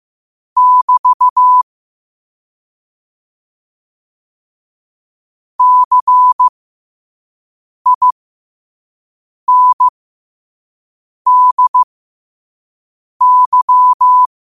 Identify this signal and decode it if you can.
Morse code, heard as a keyed tone, about 15 wpm, spaced as for 5 wpm, 1000 Hz. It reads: <BT> CINDY
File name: qotdfiles / 5wpm